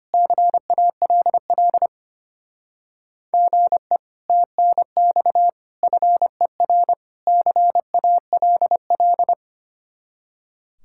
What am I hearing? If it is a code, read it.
Morse code, heard as a keyed tone, about 30 wpm, 700 Hz. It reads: CALL GE TNX FER CALL
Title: Speed-Racing Common QSO Phrases Encore - 25wpm